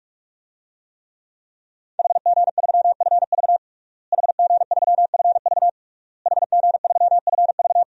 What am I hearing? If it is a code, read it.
Morse code, heard as a keyed tone, about 46 wpm, 700 Hz. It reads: HG3FV HG3FV HG3FV
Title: DXCC-100 Call Signs - 5 Characters - Repeated 3x - 45wpm